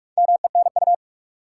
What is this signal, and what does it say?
Morse code, heard as a keyed tone, about 47 wpm, 700 Hz. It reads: MENU
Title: Four Letter Words - 45wpm